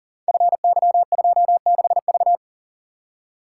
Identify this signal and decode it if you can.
Morse code, heard as a keyed tone, about 42 wpm, 700 Hz. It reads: FY26V
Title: Code Groups - 5 Letter:Numbers - 40wpm